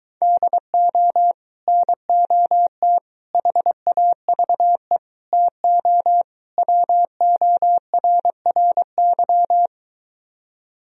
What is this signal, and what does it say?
Morse code, heard as a keyed tone, about 23 wpm, 700 Hz. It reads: DO NOT HAVE TO WORRY